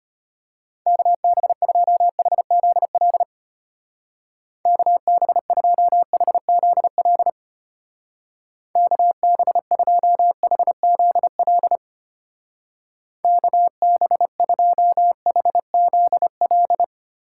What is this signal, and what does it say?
Morse code, heard as a keyed tone, about 32 wpm, 700 Hz. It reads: KB2HZL KB2HZL KB2HZL KB2HZL